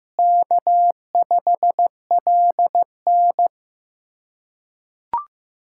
Morse code, heard as a keyed tone, about 15 wpm, 700 Hz. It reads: K5LN
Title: K1USN SST QSO Practice - 15wpm